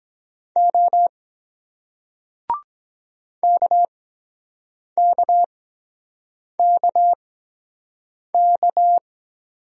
Morse code, heard as a keyed tone, about 23 wpm, spaced as for 7 wpm, 700 Hz. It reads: O KKKK